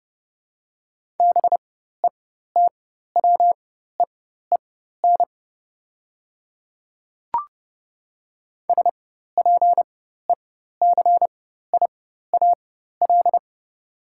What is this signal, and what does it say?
Morse code, heard as a keyed tone, about 30 wpm, spaced as for 14 wpm, 700 Hz. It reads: BETWEEN SPECIAL
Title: 4x Character Spacing - Top 500 Words - 30wpm